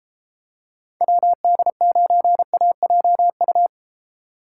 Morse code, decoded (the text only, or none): WD9AJU